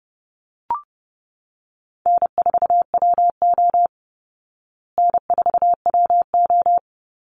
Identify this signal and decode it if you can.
Morse code, heard as a keyed tone, about 30 wpm, 700 Hz. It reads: N4WO N4WO
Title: Call Signs Repeated 2x - 30wpm